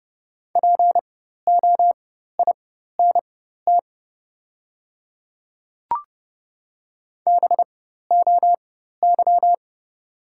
Morse code, heard as a keyed tone, about 30 wpm, spaced as for 14 wpm, 700 Hz. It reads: POINT BOY
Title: 4x Character Spacing - Top 200 Words - 30wpm